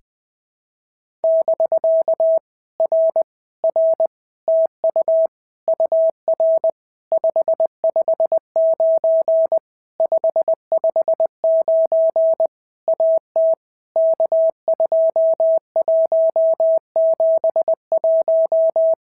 Morse code, heard as a keyed tone, about 20 wpm, 650 Hz. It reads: <BK> R R TU UR 559 559 AT K2171
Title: POTA QSO Practice - 20wpm